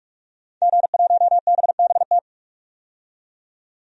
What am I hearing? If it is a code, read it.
Morse code, heard as a keyed tone, about 46 wpm, 700 Hz. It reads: G1BBT
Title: DXCC-100 Call Signs - 5 Characters - 45wpm